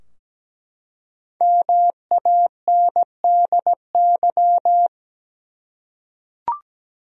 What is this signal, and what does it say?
Morse code, heard as a keyed tone, about 17 wpm, 700 Hz. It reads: MANDY
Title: USA Common Names - Max 5 Letters - 17wpm